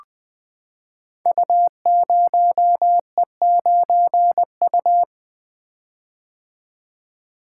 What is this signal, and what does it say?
Morse code, heard as a keyed tone, about 20 wpm, 700 Hz. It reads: U0E9U
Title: Code Groups - 5 Letter:Numbers - 20wpm